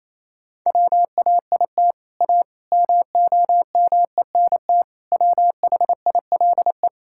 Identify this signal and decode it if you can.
Morse code, heard as a keyed tone, about 28 wpm, 700 Hz. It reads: WAIT A MOMENT WHILE
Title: Sentences from Top 1000 Words - Review Part 5 - 28wpm